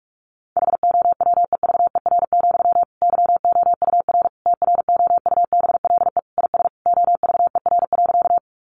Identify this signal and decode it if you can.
Morse code, heard as a keyed tone, about 44 wpm, 700 Hz. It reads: HOWEVER, YOUR TROUBLE IS OVER.
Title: Sentences from Top 1200 Words 45wpm